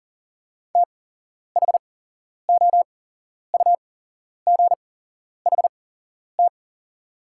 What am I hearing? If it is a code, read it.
Morse code, heard as a keyed tone, about 38 wpm, spaced as for 11 wpm, 690 Hz. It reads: THOUGHT
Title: 8x Character Spacing - Top 500 Words - 40wpm